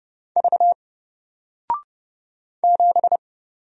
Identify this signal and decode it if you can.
Morse code, heard as a keyed tone, about 30 wpm, 700 Hz. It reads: V 7
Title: Single Character (Letter:Number:Punctuation:Prosign) Alt 30wpm